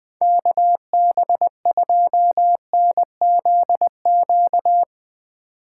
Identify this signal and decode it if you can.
Morse code, heard as a keyed tone, about 22 wpm, 700 Hz. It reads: KB2NZQ
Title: Speed Racing - USA 2x3 Call Signs 20wpm